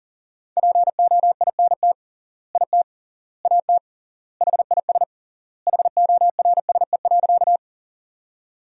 Morse code, heard as a keyed tone, about 40 wpm, 700 Hz. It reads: POINT IT AT HIS HORSE.
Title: 3x Word-Spacing - Sentences from Top 300 Words - 40wpm